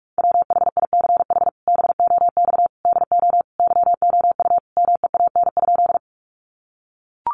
Morse code, heard as a keyed tone, about 44 wpm, 700 Hz. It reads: WHICH BOX DO YOU MEAN?
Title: Sentences from Top 500 Words 45wpm